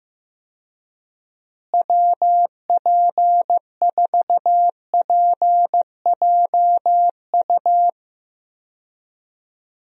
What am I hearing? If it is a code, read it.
Morse code, heard as a keyed tone, about 16 wpm, 700 Hz. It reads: WP4PJU
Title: USA Call Signs - 6 Characters - Repeated 2x - 15wpm